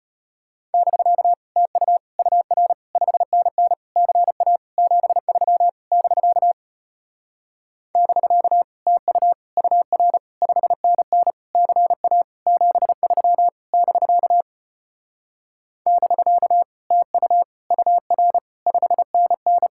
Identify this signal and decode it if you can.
Morse code, heard as a keyed tone, about 33 wpm, 700 Hz. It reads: <BK> TU UR 5NN CA 73 <BK> <BK> TU UR 5NN CA 73 <BK> <BK> TU UR 5NN